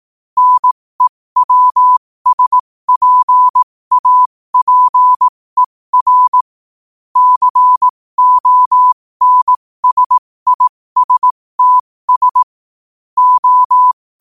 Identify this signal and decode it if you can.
Morse code, heard as a keyed tone, about 18 wpm, spaced as for 15 wpm, 1000 Hz. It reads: NEWSPAPER CONSISTS O
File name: qotdfiles / 15wpm